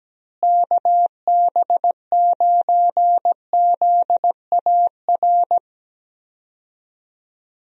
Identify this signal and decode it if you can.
Morse code, heard as a keyed tone, about 17 wpm, 700 Hz. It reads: KB9ZAR
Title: USA Call Signs - 6 Characters - Repeated 3x - 17wpm